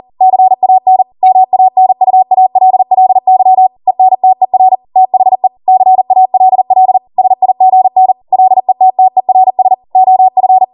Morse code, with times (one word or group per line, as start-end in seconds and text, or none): 0.2-1.0 CAN
1.2-3.7 MANUALLY
3.9-4.8 ENTER
5.0-5.5 THE
5.7-7.0 CALL
7.2-8.1 SIGN
8.3-9.7 LETTERS
9.9-10.6 OF